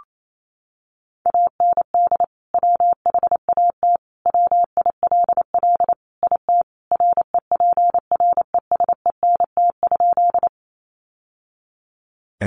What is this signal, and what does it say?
Morse code, heard as a keyed tone, about 28 wpm, 700 Hz. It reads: AND WHAT WILL IT REPRESENT?
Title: Sentences from Top 1200 Words 28wpm